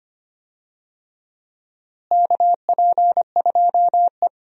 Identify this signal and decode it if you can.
Morse code, heard as a keyed tone, about 25 wpm, 700 Hz. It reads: KP2E
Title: DXCC-100 Call Signs - 4 Characters - Repeated 3x - 25wpm